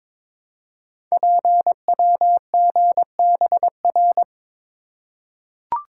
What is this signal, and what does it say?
Morse code, heard as a keyed tone, about 22 wpm, 700 Hz. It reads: PWGBR
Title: Code Groups - 5 Letters - 22wpm